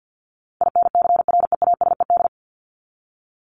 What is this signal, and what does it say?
Morse code, heard as a keyed tone, about 50 wpm, 700 Hz. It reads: INCREASED
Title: ED Suffix - Top 100 Words with Base Words 50wpm